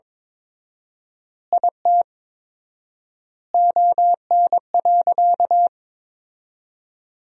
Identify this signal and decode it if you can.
Morse code, heard as a keyed tone, about 22 wpm, 700 Hz. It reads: IT ON.
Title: 4x Word-Spacing - Sentences from Top 200 Words - 22wpm